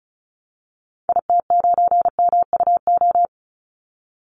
Morse code, heard as a keyed tone, about 34 wpm, 700 Hz. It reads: IT9MUO
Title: Call Signs Repeated 2x - 35wpm